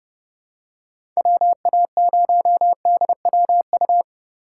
Morse code, heard as a keyed tone, about 30 wpm, 700 Hz. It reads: WA0DWU
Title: USA Call Signs - 6 Characters - 3x Character Spacing - 30wpm